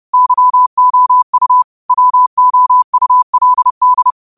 Morse code, heard as a keyed tone, about 30 wpm, 1000 Hz. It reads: YOU WOULD